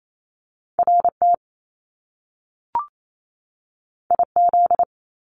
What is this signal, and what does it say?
Morse code, heard as a keyed tone, about 28 wpm, 700 Hz. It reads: RT IZ
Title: Top Two Letter Combinations in Words 28wpm